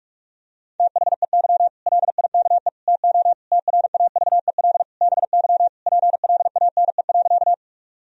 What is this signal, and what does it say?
Morse code, heard as a keyed tone, about 48 wpm, 700 Hz. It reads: THEY LIKE TO TRAVEL BY PLANE.
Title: Sentences from Top 1000 Words - Review Part 3 - 45wpm